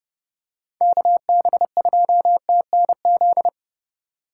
Morse code, heard as a keyed tone, about 29 wpm, 700 Hz. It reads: KB2TNZ